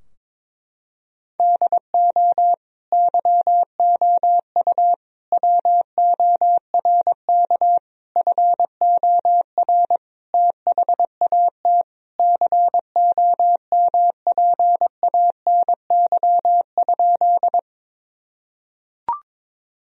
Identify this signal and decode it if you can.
Morse code, heard as a keyed tone, about 22 wpm, 700 Hz. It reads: DO YOU WORK FOR THAT COMPANY?
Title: Sentences from Top 1000 Words - Review Part 3 - 22wpm